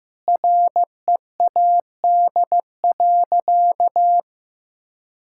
Morse code, heard as a keyed tone, about 15 wpm, 700 Hz. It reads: READ.
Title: Sentences from Top 1000 Words - Review Part 5 - 15wpm